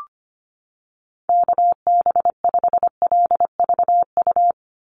Morse code, heard as a keyed tone, about 25 wpm, 700 Hz. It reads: KB5LVU
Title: USA Call Signs 25wpm